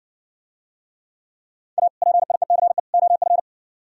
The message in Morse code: A PIECE OF

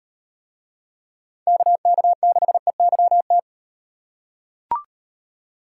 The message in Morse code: KK6EYT

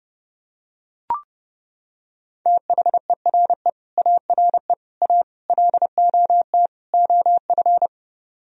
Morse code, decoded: THERE ARE A LOT OF

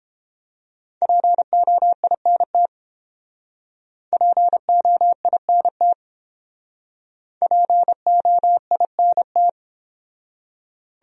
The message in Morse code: POINT POINT POINT

